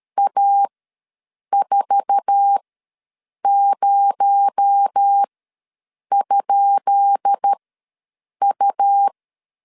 Morse code, heard as a keyed tone, about 13 wpm, spaced as for 7 wpm, 800 Hz. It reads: A40?U
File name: LEZIONI / 10 SIGNS / 10C DOT ws 12wpm 3548